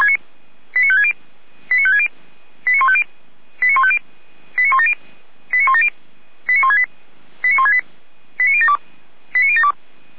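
Five-tone selective call (ZVEI), heard as quick runs of five tones, tones of about 70 ms. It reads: 89560 89560 89150 89150 89170 89170 89168 89168 89072 89072
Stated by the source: ZVEI five-tone call received by SDR, archived on the Feuerwehr Orlamünde server.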